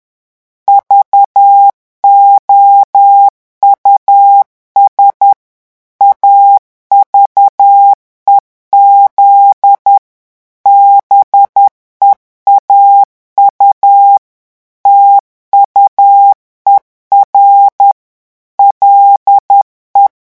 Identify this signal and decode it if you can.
Morse code, heard as a keyed tone, about 10 wpm, 800 Hz. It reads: VOUS AVEZ BEAU TUER LE
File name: morse mp3 / revision francais 10 mots minute